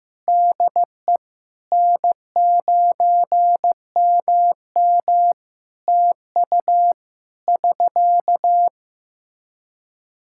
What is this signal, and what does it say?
Morse code, heard as a keyed tone, about 15 wpm, 700 Hz. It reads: DE N9MM TU <SK>
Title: K1USN SST QSO Practice 15wpm